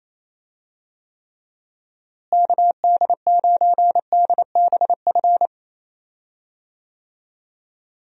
Morse code, heard as a keyed tone, about 28 wpm, 700 Hz. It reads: KD9DBF
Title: USA Call Signs - 6 Characters - Repeated 2x - 28wpm